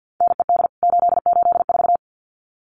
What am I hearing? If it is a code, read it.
Morse code, heard as a keyed tone, about 50 wpm, 700 Hz. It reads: NED 884
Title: CWT Exchange 50wpm